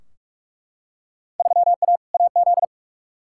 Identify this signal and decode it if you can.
Morse code, heard as a keyed tone, about 43 wpm, 700 Hz. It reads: 3A AZ
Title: Field Day Exchanges - 45wpm